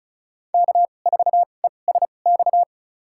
Morse code, heard as a keyed tone, about 36 wpm, spaced as for 25 wpm, 700 Hz. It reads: K4ESX
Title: USA Call Signs - 5 Characters - 2x Character Spacing - 35wpm